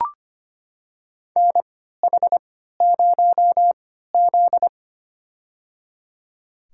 Morse code, heard as a keyed tone, about 25 wpm, spaced as for 14 wpm, 700 Hz. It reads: NH0Z
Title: USA Call Signs - 4 Characters - 3x Character Spacing - 25wpm